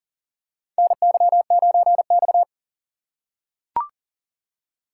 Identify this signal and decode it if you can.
Morse code, heard as a keyed tone, about 40 wpm, 700 Hz. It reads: NY9X
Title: K1USN SST QSO Practice - 40wpm